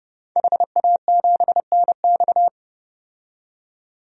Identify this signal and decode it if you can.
Morse code, heard as a keyed tone, about 30 wpm, 700 Hz. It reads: HA7NX
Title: DXCC-100 Call Signs - 5 Characters - 30wpm